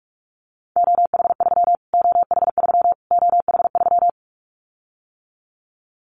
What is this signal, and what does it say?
Morse code, heard as a keyed tone, about 44 wpm, 700 Hz. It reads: OH3 OH3 OH3
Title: Callsign Prefixes for Top 25 DXCC - Repeated 3x - 45wpm